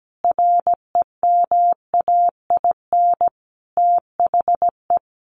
Morse code, heard as a keyed tone, about 17 wpm, 700 Hz. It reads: REMAIN THE
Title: Sentences from Top 1200 Words 17wpm